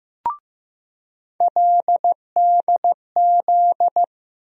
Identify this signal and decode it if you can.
Morse code, heard as a keyed tone, about 16 wpm, 700 Hz. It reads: LDZ
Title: USA Call Sign Postfixes - 3 Letters 15wpm